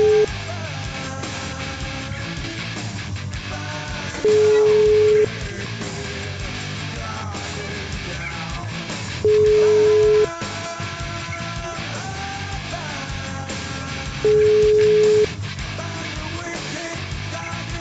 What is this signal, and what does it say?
Ringback tone (Europe, 425 Hz), with noise.